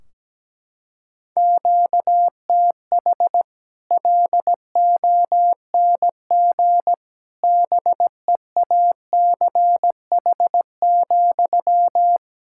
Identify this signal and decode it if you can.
Morse code, heard as a keyed tone, about 17 wpm, 700 Hz. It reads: QTH LONG BEACH,
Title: Practice QSO - Exchange 1 - Beginning - 17wpm